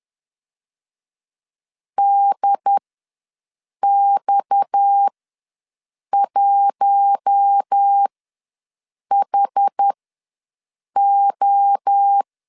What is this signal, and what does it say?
Morse code, heard as a keyed tone, about 10 wpm, spaced as for 6 wpm, 800 Hz. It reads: DX1HO
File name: LEZIONI / 6 PKX0 / 6 PKX ws 10wpm 3706